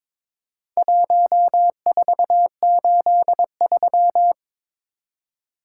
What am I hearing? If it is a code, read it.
Morse code, heard as a keyed tone, about 22 wpm, 700 Hz. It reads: 1483
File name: practice / 22wpm / Four Letter-Number Combinations - 22wpm